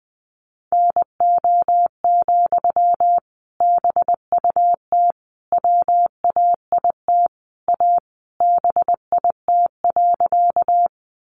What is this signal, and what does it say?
Morse code, heard as a keyed tone, about 20 wpm, 700 Hz. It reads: NO, BUT WAIT A BIT.
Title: Sentences from Top 900 Words 20wpm